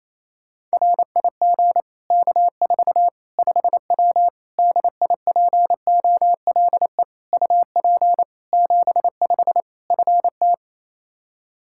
Morse code, heard as a keyed tone, about 28 wpm, 700 Hz. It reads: RIG K4 5W DIPOLE UP 75 FT
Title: Practice QSO - Exchange 2 - Info - 28wpm